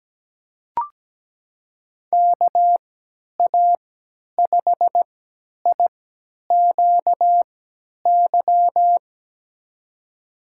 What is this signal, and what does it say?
Morse code, heard as a keyed tone, about 17 wpm, spaced as for 10 wpm, 700 Hz. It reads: KA5IQY
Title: USA Call Signs - 6 Characters - 3x Character Spacing - 17wpm